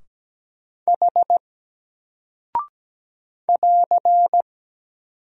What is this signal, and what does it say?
Morse code, heard as a keyed tone, about 17 wpm, 700 Hz. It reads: H <AR>
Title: Single Character (Letter:Number:Punctuation:Prosign) 17wpm